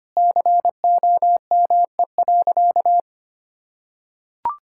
Morse code, heard as a keyed tone, about 25 wpm, 700 Hz. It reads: COME.
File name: practice / 25wpm / Sentences with 2-Word Phrases - 25wpm